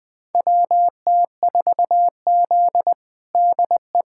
Morse code, heard as a keyed tone, about 20 wpm, 700 Hz. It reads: WT4Z DE
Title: USA Callsign DE Callsign - 4 Characters - 20wpm